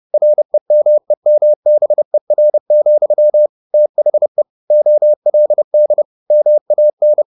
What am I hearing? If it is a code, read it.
Morse code, heard as a keyed tone, about 30 wpm, 600 Hz. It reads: REMEMBER, THE OLD MAN
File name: the old man and the sea / the old man and the sea 1 30wpm